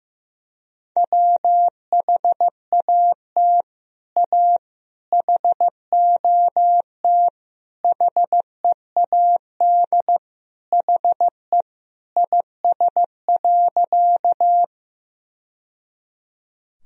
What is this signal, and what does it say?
Morse code, heard as a keyed tone, about 15 wpm, 700 Hz. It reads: WHAT A HOT HEAD HE IS.
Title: Sentences from Top 1000 Words - Review Part 3 - 15wpm